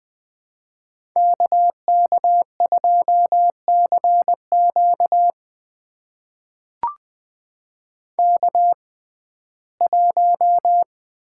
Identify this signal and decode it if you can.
Morse code, heard as a keyed tone, about 20 wpm, spaced as for 14 wpm, 700 Hz. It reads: KK2CQ K1
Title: USA Call Signs - 5 Characters - 6x Character Spacing - 20wpm